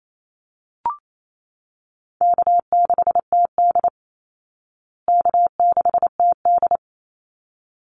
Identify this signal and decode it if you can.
Morse code, heard as a keyed tone, about 28 wpm, 700 Hz. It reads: K6TD K6TD
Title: Call Signs Repeated 3x - 28wpm